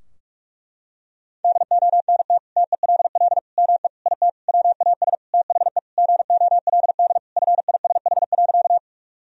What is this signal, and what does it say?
Morse code, heard as a keyed tone, about 44 wpm, 700 Hz. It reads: DONT TELL ME IT WAS THE GOLD FISH.
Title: Sentences from Top 1000 Words - Review Part 3 - 45wpm